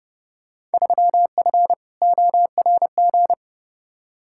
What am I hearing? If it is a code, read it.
Morse code, heard as a keyed tone, about 30 wpm, 700 Hz. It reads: 3F ORG
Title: Field Day Exchanges - 30wpm